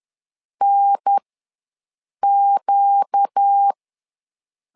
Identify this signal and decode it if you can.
Morse code, heard as a keyed tone, about 10 wpm, spaced as for 6 wpm, 800 Hz. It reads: NQ
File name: LEZIONI / 4 Q ws 10wpm 2306